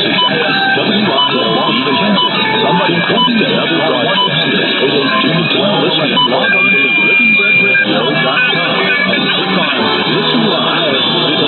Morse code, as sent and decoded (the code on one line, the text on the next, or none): none